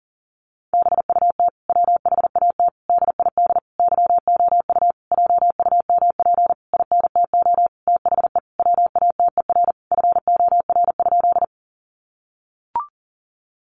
Morse code, heard as a keyed tone, about 40 wpm, 700 Hz. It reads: BUT WHAT DID YOU JUMP INTO THE WATER FOR?
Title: Sentences from Top 1300 Words 40wpm